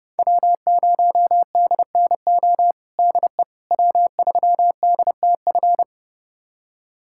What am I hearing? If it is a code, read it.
Morse code, heard as a keyed tone, about 31 wpm, 700 Hz. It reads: W0DNO DE W3DTF
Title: USA Callsign DE Callsign - 5 Characters - 30wpm